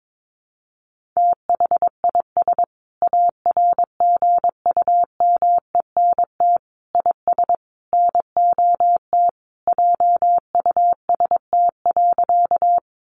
Morse code, heard as a keyed tone, about 22 wpm, 700 Hz. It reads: THIS ARGUMENT IS NOT JUST.
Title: Sentences from Top 1400 Words 22wpm